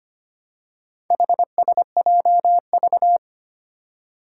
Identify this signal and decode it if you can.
Morse code, heard as a keyed tone, about 25 wpm, 700 Hz. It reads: HSJV
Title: Four Challenging Letter-Number Combinations 25wpm